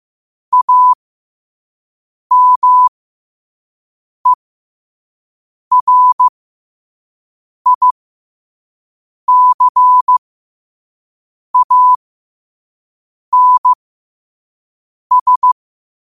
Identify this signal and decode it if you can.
Morse code, heard as a keyed tone, about 15 wpm, spaced as for 5 wpm, 1000 Hz. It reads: AMERICANS
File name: qotdfiles / 5wpm